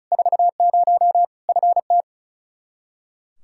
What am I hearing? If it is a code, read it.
Morse code, heard as a keyed tone, about 36 wpm, 700 Hz. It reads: 40 FT
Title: Practice QSO - Exchange 2 - Info - 35wpm